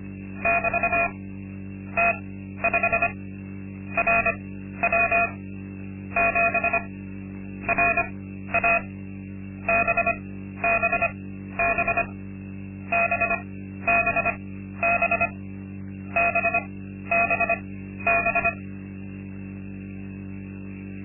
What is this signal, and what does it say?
Morse code, heard as a keyed tone, about 25 wpm, spaced as for 12 wpm, 680 Hz. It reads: <BT> T5 RW 7 RA BBB BBB BBB
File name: Linksmakalnis-2